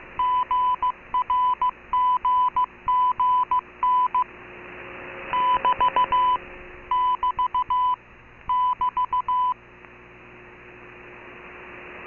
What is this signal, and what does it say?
Morse code, heard as a keyed tone, about 15 wpm, 1000 Hz. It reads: GRGGN <BT> <BT> <BT>